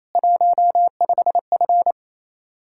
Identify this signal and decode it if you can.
Morse code, heard as a keyed tone, about 36 wpm, 700 Hz. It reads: 15F